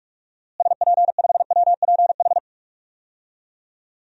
Morse code, heard as a keyed tone, about 46 wpm, 700 Hz. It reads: SP5WPH
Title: DXCC-100 Call Signs - 6 Characters - Repeated 3x - 45wpm